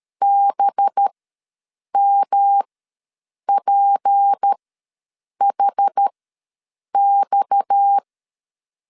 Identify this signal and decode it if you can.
Morse code, heard as a keyed tone, about 13 wpm, spaced as for 7 wpm, 800 Hz. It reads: BMPHX
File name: LEZIONI / 6 PKX0 / 6 PKX ws 12wpm 3106